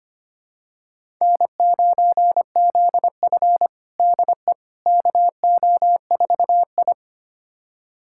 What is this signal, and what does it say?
Morse code, heard as a keyed tone, about 25 wpm, 700 Hz. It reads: N9ZF DE KO4I
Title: USA Callsign DE Callsign - 4 Characters - 25wpm